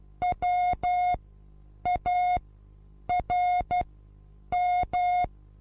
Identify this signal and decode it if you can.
Morse code, heard as a keyed tone, about 12 wpm, spaced as for 8 wpm, 720 Hz. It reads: WARM
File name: Onlinelehrgang / morsen / DJ4UF Morsen23